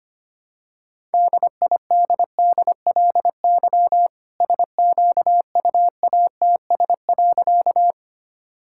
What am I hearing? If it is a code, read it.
Morse code, heard as a keyed tone, about 25 wpm, 700 Hz. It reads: DIDDLY SQUATS.
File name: practice / 25wpm / All-Age Appropriate Jokes - 25wpm